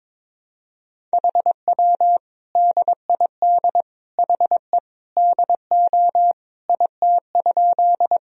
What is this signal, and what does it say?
Morse code, heard as a keyed tone, about 22 wpm, 700 Hz. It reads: HW DID HE DO IT?
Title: Sentences from Top 100 Words and Abbreviations 22wpm